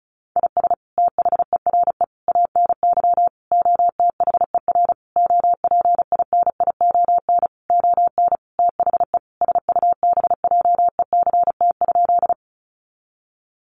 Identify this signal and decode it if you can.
Morse code, heard as a keyed tone, about 34 wpm, 700 Hz. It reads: IS THERE ANY OTHER OPINION ON THE SUBJECT?
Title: Sentences from Top 1200 Words 35wpm